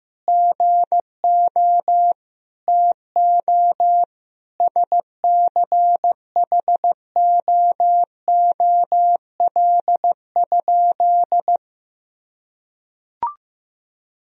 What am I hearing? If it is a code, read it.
Morse code, heard as a keyed tone, about 15 wpm, 700 Hz. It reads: GO TO SCHOOL?
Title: Sentences from Top 1000 Words - Review Part 1 - 15wpm